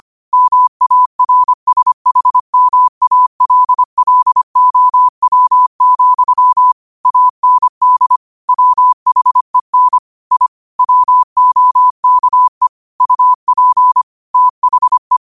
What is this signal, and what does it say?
Morse code, heard as a keyed tone, about 25 wpm, 1000 Hz. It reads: MARSHMALLOW, AND WHEN I WOKE UP THE